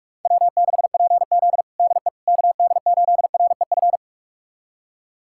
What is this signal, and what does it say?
Morse code, heard as a keyed tone, about 46 wpm, 700 Hz. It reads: W6PZ DE KD8REF
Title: USA Callsign DE Callsign 45wpm